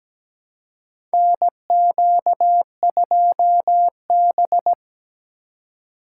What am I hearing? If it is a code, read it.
Morse code, heard as a keyed tone, about 17 wpm, 700 Hz. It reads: NQ2B
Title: USA Call Signs - 4 Characters - 2x Character Spacing - 17wpm